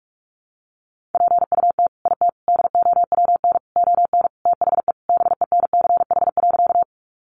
Morse code, heard as a keyed tone, about 44 wpm, 700 Hz. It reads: PUT IT DOWN ON THE BENCH.
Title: Sentences from Top 1400 Words 45wpm